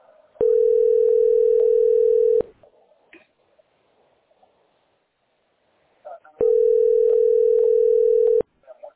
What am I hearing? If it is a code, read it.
Ringback tone (North America), 440 + 480 Hz.
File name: 32dc8e6e-70ed-4ea3-9327